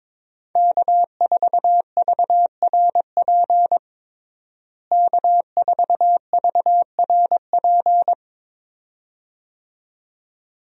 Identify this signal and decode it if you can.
Morse code, heard as a keyed tone, about 21 wpm, 700 Hz. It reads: K4VRP K4VRP
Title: USA Call Signs - 5 Characters - Repeated 2x - 22wpm